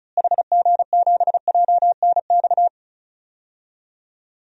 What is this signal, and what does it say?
Morse code, heard as a keyed tone, about 36 wpm, 700 Hz. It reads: HG7JNX
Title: DXCC-100 Call Signs - 6 Characters - Repeated 3x - 35wpm